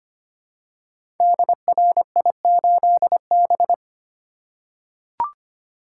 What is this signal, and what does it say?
Morse code, heard as a keyed tone, about 25 wpm, 700 Hz. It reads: DRI8B
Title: Code Groups - 5 Letter:Numbers - 25wpm